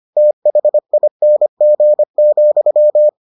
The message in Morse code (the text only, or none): THING,